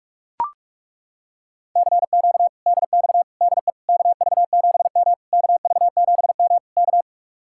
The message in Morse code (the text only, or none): CQ DX DE KV7M KV7M K